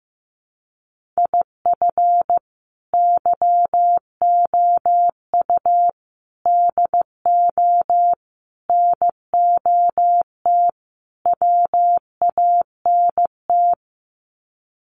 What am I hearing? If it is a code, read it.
Morse code, heard as a keyed tone, about 15 wpm, 700 Hz. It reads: IF YOU DO NOT WANT
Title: Sets of 5 Words 15wpm